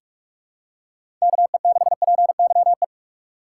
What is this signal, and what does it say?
Morse code, heard as a keyed tone, about 42 wpm, 700 Hz. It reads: KE6PYE
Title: USA Call Signs - 6 Characters - 4x Character Spacing - 45wpm